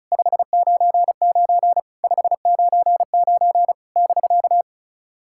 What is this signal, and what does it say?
Morse code, heard as a keyed tone, about 38 wpm, 700 Hz. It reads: 599 599 <BK>
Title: SOTA QSO Practice 35wpm